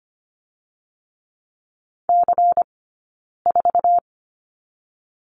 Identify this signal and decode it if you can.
Morse code, heard as a keyed tone, about 25 wpm, 700 Hz. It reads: C 4
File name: course / 25:10wpm / 092 Lesson 16d - Letter c - Two Characters 25wpm